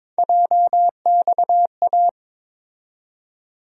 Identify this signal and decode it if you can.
Morse code, heard as a keyed tone, about 22 wpm, 700 Hz. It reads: JXA
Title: USA Call Sign Postfixes - 3 Letters 22wpm